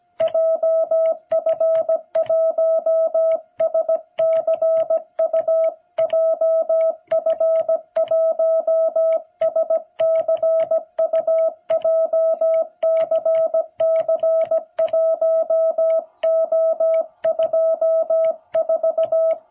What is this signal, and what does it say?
Morse code, heard as a keyed tone, about 17 wpm, 660 Hz. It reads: JF1SCU JF1SCU JCC1O24
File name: jh8cbh / 2021cw